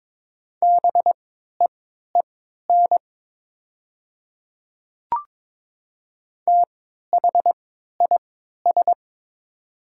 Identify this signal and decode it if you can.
Morse code, heard as a keyed tone, about 22 wpm, spaced as for 12 wpm, 700 Hz. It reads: BEEN THIS
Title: 3x Character Spacing - Top 500 Words - 22wpm